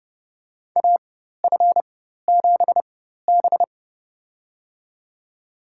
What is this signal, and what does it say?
Morse code, heard as a keyed tone, about 31 wpm, spaced as for 14 wpm, 700 Hz. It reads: AF7B
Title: USA Call Signs - 4 Characters - 4x Character Spacing - 30wpm